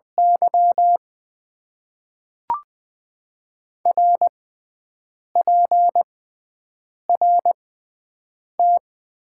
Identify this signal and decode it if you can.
Morse code, heard as a keyed tone, about 20 wpm, spaced as for 5 wpm, 700 Hz. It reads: YR PRT
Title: Top 90 QSO Elements - 6x Character Spacing - 20wpm